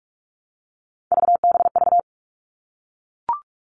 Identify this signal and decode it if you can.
Morse code, heard as a keyed tone, about 44 wpm, 700 Hz. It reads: VBV
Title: All Cut Numbers - 3-Digit 45wpm